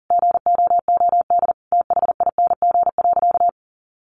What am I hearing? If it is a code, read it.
Morse code, heard as a keyed tone, about 40 wpm, 700 Hz. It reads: GOOD THING.
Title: Sentences from Top 1500 Words 40wpm